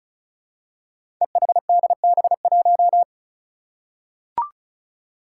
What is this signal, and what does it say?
Morse code, heard as a keyed tone, about 36 wpm, 700 Hz. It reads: EHDB1
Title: Five Challenging Letter-Number Combinations 35wpm